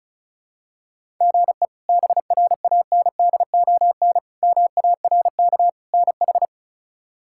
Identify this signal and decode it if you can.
Morse code, heard as a keyed tone, about 36 wpm, 700 Hz. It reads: GE BRANDON MARK NH
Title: K1USN SST QSO Practice - 35wpm